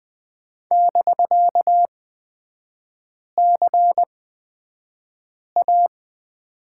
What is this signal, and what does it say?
Morse code, heard as a keyed tone, about 20 wpm, spaced as for 5 wpm, 700 Hz. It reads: <BK>CA
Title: 250 Lesson 40e - BK - Three Characters 20wpm